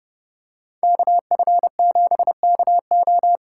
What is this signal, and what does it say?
Morse code, heard as a keyed tone, about 29 wpm, 700 Hz. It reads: KF7KO